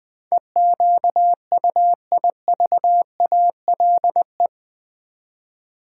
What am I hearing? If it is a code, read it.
Morse code, heard as a keyed tone, about 20 wpm, 700 Hz. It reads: EQUIVALE
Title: Build A Word - 4 Letters At A Time - 10-Letter Words - 20wpm